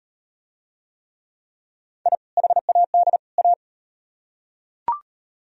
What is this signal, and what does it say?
Morse code, heard as a keyed tone, about 35 wpm, 700 Hz. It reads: I HAD A